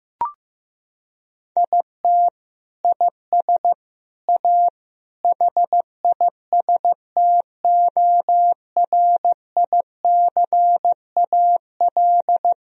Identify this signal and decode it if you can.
Morse code, heard as a keyed tone, about 15 wpm, 700 Hz. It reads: IT IS A HISTORICAL
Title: Sentences from Top 1000 Words - Review Part 4 - 15wpm